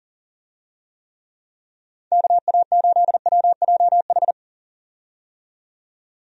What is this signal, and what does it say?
Morse code, heard as a keyed tone, about 40 wpm, 700 Hz. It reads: KA8WJH
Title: USA Call Signs - 6 Characters - 40wpm